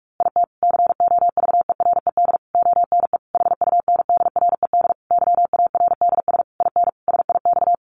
Morse code, heard as a keyed tone, about 44 wpm, 700 Hz. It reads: IT COVERED ONE HUNDRED YARDS IN SIX
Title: A Sound Of Thunder 45wpm